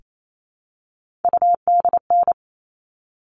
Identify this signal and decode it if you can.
Morse code, heard as a keyed tone, about 28 wpm, 700 Hz. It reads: UDN
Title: All Cut Numbers - 3-Digit 28wpm